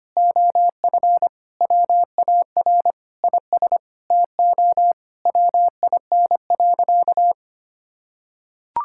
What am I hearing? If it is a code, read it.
Morse code, heard as a keyed tone, about 25 wpm, 700 Hz. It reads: OF WAR IS TO WIN.
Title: Sentences from Top 1600 Words 25wpm